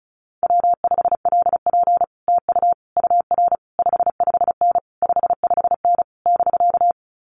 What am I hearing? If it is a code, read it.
Morse code, heard as a keyed tone, about 34 wpm, 700 Hz. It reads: W5LP TU UR 55N 55N <BK>
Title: Common QSO Phrases Encore 35wpm